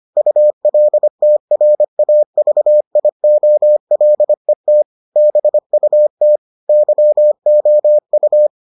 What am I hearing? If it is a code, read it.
Morse code, heard as a keyed tone, about 25 wpm, 600 Hz. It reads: ULTRAVIOLET BUT YOU